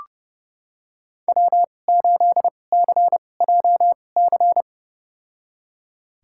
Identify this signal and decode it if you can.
Morse code, heard as a keyed tone, about 30 wpm, spaced as for 22 wpm, 700 Hz. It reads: W8CJC
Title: USA Call Signs - 5 Characters - 2x Character Spacing - 30wpm